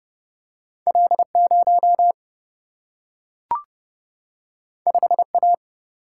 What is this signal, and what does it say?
Morse code, heard as a keyed tone, about 30 wpm, 700 Hz. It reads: L 0 5 A
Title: Single-Letter Single-Number 30wpm